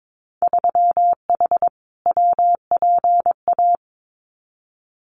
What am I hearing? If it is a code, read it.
Morse code, heard as a keyed tone, about 22 wpm, 700 Hz. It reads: 3H WPA
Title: Winter Field Day Exchanges Encore 22wpm